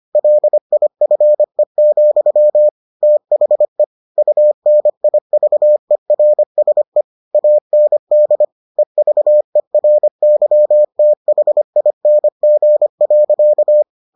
Morse code, heard as a keyed tone, about 25 wpm, 600 Hz. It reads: LIFE, THE UNIVERSE AND EVERYTHING.